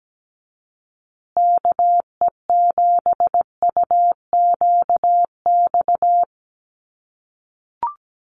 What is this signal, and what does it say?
Morse code, heard as a keyed tone, about 17 wpm, 700 Hz. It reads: KE7UQX
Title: USA Call Signs Repeated 2x 17wpm